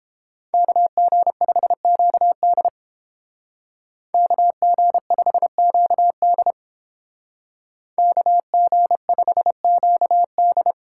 Morse code, heard as a keyed tone, about 29 wpm, 700 Hz. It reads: KG5QD KG5QD KG5QD